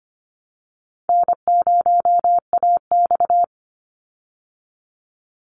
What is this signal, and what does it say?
Morse code, heard as a keyed tone, about 25 wpm, 700 Hz. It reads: N0AX
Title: Call Signs Repeated 2x - 25wpm